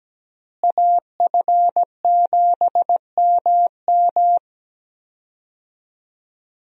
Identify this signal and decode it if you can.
Morse code, heard as a keyed tone, about 17 wpm, 700 Hz. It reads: AF7MM
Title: USA Call Signs - 5 Characters - Repeated 2x - 17wpm